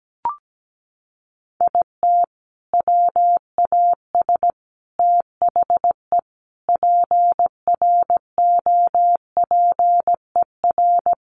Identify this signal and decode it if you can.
Morse code, heard as a keyed tone, about 17 wpm, 700 Hz. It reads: IT WAS THE PROPER
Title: Sentences from Top 1300 Words 17wpm